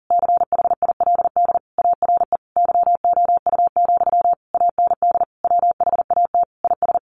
Morse code, heard as a keyed tone, about 40 wpm, 700 Hz. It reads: CHILD ARE YOU, AND WHAT IS
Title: Sentences from Top 1200 Words 40wpm